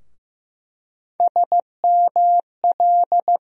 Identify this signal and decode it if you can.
Morse code, heard as a keyed tone, about 15 wpm, 700 Hz. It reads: SML
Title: USA Call Sign Postfixes - 3 Letters 15wpm